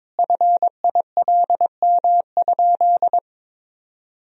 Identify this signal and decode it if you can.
Morse code, heard as a keyed tone, about 22 wpm, 700 Hz. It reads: FILM?
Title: Sentences from Top 1000 Words - Review Part 3 - 22wpm